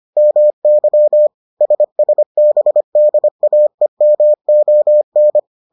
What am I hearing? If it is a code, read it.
Morse code, heard as a keyed tone, about 25 wpm, 600 Hz. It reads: MY SSBDAEMON